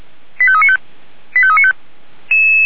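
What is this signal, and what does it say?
Five-tone selective call (ZVEI), heard as quick runs of five tones, tones of about 70 ms. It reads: 85395 85395
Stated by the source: Five-tone ZVEI selective call received by SDR.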